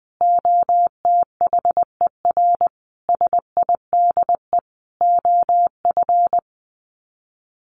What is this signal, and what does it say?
Morse code, heard as a keyed tone, about 19 wpm, 700 Hz. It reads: OTHER SIDE OF
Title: Sets of 5 Words Encore 20wpm